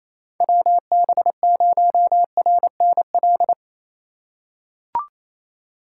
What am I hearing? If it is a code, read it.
Morse code, heard as a keyed tone, about 27 wpm, 700 Hz. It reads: WB0RNL
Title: USA Call Signs - 6 Characters - 3x Character Spacing - 28wpm